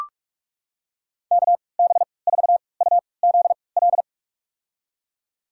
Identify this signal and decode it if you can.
Morse code, heard as a keyed tone, about 45 wpm, spaced as for 25 wpm, 700 Hz. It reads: KB4UZL
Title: USA Call Signs - 6 Characters - 3x Character Spacing - 45wpm